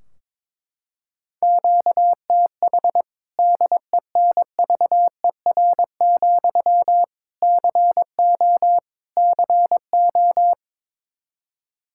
Morse code, heard as a keyed tone, about 22 wpm, 700 Hz. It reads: QTH DENVER, CO CO
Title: Practice QSO - Exchange 1 - Beginning - 22wpm